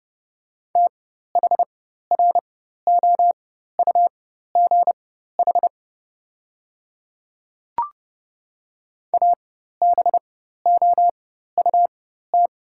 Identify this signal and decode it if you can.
Morse code, heard as a keyed tone, about 30 wpm, spaced as for 14 wpm, 700 Hz. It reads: THROUGH ABOUT
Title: 4x Character Spacing - Top 300 Words - 30wpm